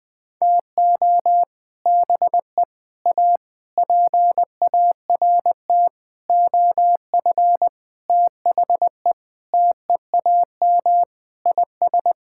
Morse code, heard as a keyed tone, about 21 wpm, 700 Hz. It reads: TO BE A PART OF THE TEAM IS
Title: Sentences with 6-Word Phrases - 20wpm